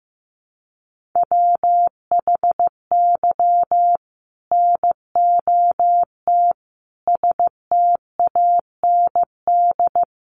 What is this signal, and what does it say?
Morse code, heard as a keyed tone, about 14 wpm, 700 Hz. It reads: WHY NOT STAND
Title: Sentences from Top 1500 Words 15wpm